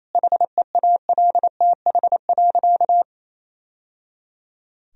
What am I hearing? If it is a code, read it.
Morse code, heard as a keyed tone, about 28 wpm, 700 Hz. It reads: HEALTH.
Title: Sentences from Top 1000 Words - Review Part 3 - 28wpm